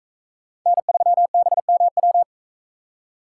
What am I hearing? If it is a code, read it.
Morse code, heard as a keyed tone, about 43 wpm, 700 Hz. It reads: N3BMW